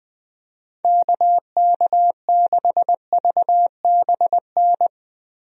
Morse code, heard as a keyed tone, about 21 wpm, 700 Hz. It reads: KK6VBN
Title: USA Call Signs - 6 Characters - Repeated 3x - 20wpm